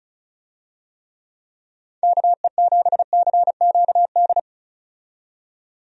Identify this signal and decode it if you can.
Morse code, heard as a keyed tone, about 36 wpm, 700 Hz. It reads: KE7CQD
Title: USA Call Signs - 6 Characters - 35wpm